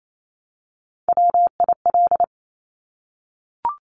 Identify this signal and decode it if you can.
Morse code, heard as a keyed tone, about 28 wpm, 700 Hz. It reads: WIL
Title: Top Three Letter Combinations in Words 28wpm